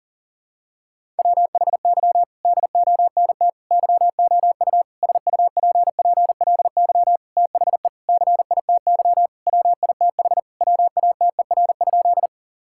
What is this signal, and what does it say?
Morse code, heard as a keyed tone, about 40 wpm, 700 Hz. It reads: WHY DONT YOU SUPPLY THE CITY WITH WATER?
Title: Sentences from Top 1000 Words - Review Part 5 - 40wpm